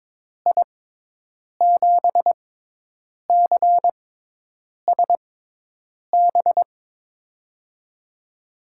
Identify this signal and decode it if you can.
Morse code, heard as a keyed tone, about 22 wpm, spaced as for 8 wpm, 700 Hz. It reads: I7CSB
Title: USA Call Signs - 6 Characters - 6x Character Spacing - 22wpm